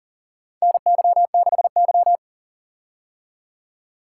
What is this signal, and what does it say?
Morse code, heard as a keyed tone, about 38 wpm, 700 Hz. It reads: NY6Y
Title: K1USN SST QSO Practice - 40wpm